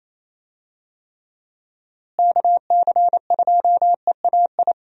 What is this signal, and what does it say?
Morse code, heard as a keyed tone, about 28 wpm, 700 Hz. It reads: KC2EAI
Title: USA Call Signs - 6 Characters - Repeated 2x - 28wpm